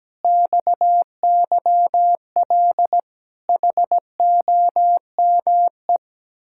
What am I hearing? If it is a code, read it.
Morse code, heard as a keyed tone, about 17 wpm, 700 Hz. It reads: XYL HOME
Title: Speed-Racing Common QSO Phrases - 17wpm